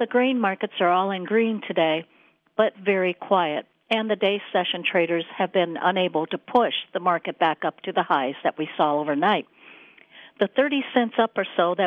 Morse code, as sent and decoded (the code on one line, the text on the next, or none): none